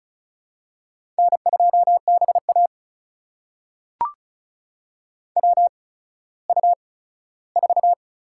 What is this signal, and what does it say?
Morse code, heard as a keyed tone, about 33 wpm, spaced as for 17 wpm, 700 Hz. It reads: N2BA WU4
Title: USA Call Signs - 4 Characters - 8x Character Spacing - 35wpm